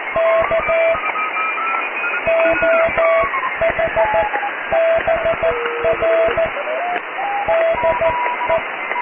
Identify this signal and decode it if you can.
Morse code, heard as a keyed tone, about 14 wpm, spaced as for 12 wpm, 660 Hz, with radio noise. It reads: K OH BR DE
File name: cw / QSO+QRM